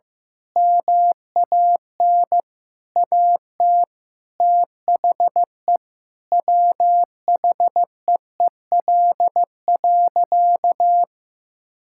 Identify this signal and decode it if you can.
Morse code, heard as a keyed tone, about 15 wpm, 700 Hz. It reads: MAN AT THE WHEEL.
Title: Sentences from Top 1000 Words - Review Part 5 - 15wpm